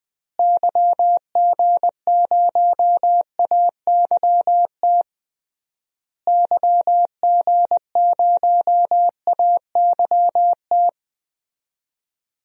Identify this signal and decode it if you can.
Morse code, heard as a keyed tone, about 21 wpm, 700 Hz. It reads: YG0AYT YG0AYT
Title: DXCC-100 Call Signs - 6 Characters - Repeated 3x - 20wpm